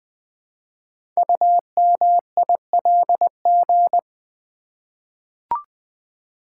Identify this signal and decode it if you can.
Morse code, heard as a keyed tone, about 20 wpm, 700 Hz. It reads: UMILG
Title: Code Groups - 5 Letters - 20wpm